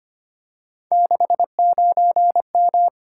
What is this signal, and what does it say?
Morse code, heard as a keyed tone, about 25 wpm, 700 Hz. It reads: 69M
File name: Three Letter-Number Combinations - 25wpm